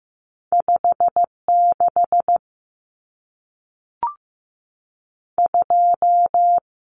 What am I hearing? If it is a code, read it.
Morse code, heard as a keyed tone, about 15 wpm, 700 Hz. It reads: 56 2
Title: Numbers - 2-Digit 15wpm